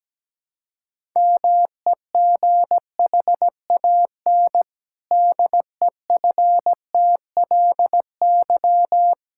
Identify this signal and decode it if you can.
Morse code, heard as a keyed tone, about 17 wpm, 700 Hz. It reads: MEGHAN DEFTLY
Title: Pangram Challenge v2 - 17wpm